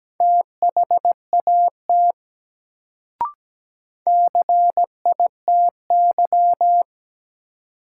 Morse code, heard as a keyed tone, about 17 wpm, 700 Hz. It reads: THAT CITY
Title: Sets of 2 Words - New Word Drill - 17wpm